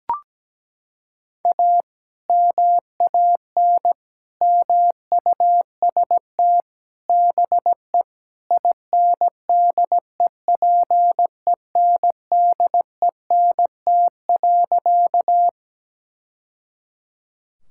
Morse code, heard as a keyed tone, about 17 wpm, 700 Hz. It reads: A MAN MUST BE INDEPENDENT.
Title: Sentences from Top 1000 Words - Review Part 6 - 17wpm